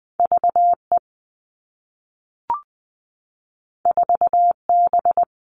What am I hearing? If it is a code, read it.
Morse code, heard as a keyed tone, about 20 wpm, 700 Hz. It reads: V E 4 B
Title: All Cut Numbers - 2-Digit 20wpm